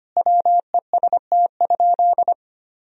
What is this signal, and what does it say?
Morse code, heard as a keyed tone, about 25 wpm, 700 Hz. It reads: WEST?
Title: Sentences from Top 1000 Words - Review Part 2 - 25wpm